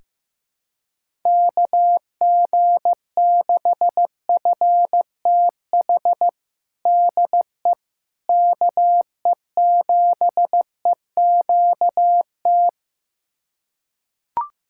Morse code, heard as a keyed tone, about 15 wpm, 700 Hz. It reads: KG6FTH DE KE7EQT
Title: USA Callsign DE Callsign - 6 Characters - 15wpm